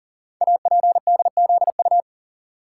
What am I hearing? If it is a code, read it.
Morse code, heard as a keyed tone, about 42 wpm, 700 Hz. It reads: APDZU
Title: Code Groups - 5 Letters - 40wpm